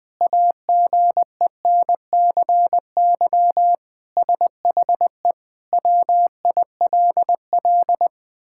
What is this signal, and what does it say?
Morse code, heard as a keyed tone, about 19 wpm, 700 Hz. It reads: AGENCY SHE WILL
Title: Sentences with 7-Word Phrases - Spoken Beginning - 20wpm